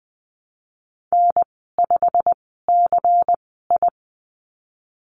Morse code, heard as a keyed tone, about 19 wpm, spaced as for 14 wpm, 700 Hz. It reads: N5CI
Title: 173 Lesson 28g - Number 9 - Call Signs 20wpm